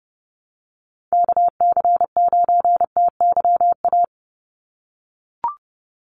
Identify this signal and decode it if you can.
Morse code, heard as a keyed tone, about 30 wpm, 700 Hz. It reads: KC9TYA
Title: NA SOTA Chaser Callsigns 30wpm